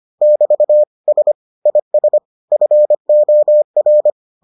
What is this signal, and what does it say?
Morse code, heard as a keyed tone, about 25 wpm, 600 Hz. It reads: <BT> S IS FOR